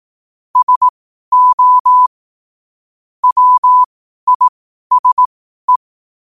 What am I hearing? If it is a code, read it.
Morse code, heard as a keyed tone, about 18 wpm, spaced as for 12 wpm, 1000 Hz. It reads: SO WISE
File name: 12wpm